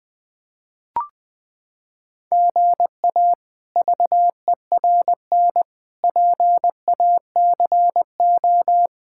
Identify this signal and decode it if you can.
Morse code, heard as a keyed tone, about 20 wpm, 700 Hz. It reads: GA VERN PACO
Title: K1USN SST QSO Practice - 20wpm